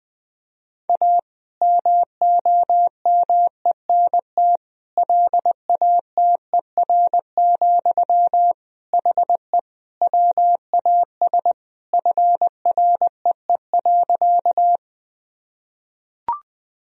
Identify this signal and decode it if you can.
Morse code, heard as a keyed tone, about 20 wpm, 700 Hz. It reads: A MOMENT LATER, HE WAS FREE.
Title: Sentences from Top 1000 Words - Review Part 4 - 20wpm